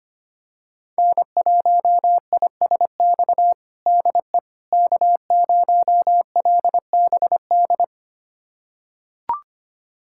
Morse code, heard as a keyed tone, about 25 wpm, 700 Hz. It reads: N1ISX DE K0LBD
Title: USA Callsign DE Callsign - 5 Characters - 25wpm